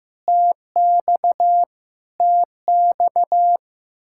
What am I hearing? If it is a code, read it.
Morse code, heard as a keyed tone, about 15 wpm, 700 Hz. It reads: TX TX
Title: Practice QSO - Exchange 1 - Beginning - 15wpm